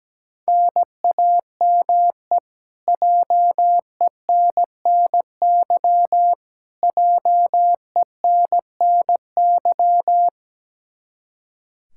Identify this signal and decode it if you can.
Morse code, heard as a keyed tone, about 17 wpm, 700 Hz. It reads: NAME JENNY JENNY
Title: Practice QSO - Exchange 1 - Beginning - 17wpm